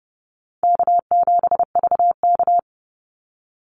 Morse code, heard as a keyed tone, about 30 wpm, 700 Hz. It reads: K7VK
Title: NA SOTA Activator Callsigns 30wpm